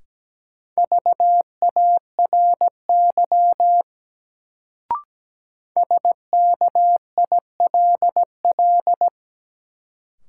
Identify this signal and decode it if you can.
Morse code, heard as a keyed tone, about 17 wpm, 700 Hz. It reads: VARY SKILL
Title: Sentences from Top 1200 Words - New Word Drill - 17wpm